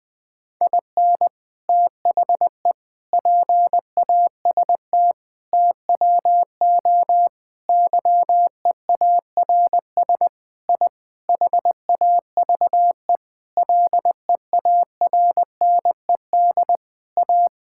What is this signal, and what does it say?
Morse code, heard as a keyed tone, about 21 wpm, 700 Hz. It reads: IN THE PAST TWO YEARS I HAVE LEARNED A
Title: Sentences with 5-Word Phrases - 20wpm